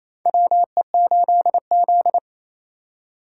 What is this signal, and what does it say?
Morse code, heard as a keyed tone, about 28 wpm, 700 Hz. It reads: WE8Z
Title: USA Call Signs - 8x Character Spacing - 28wpm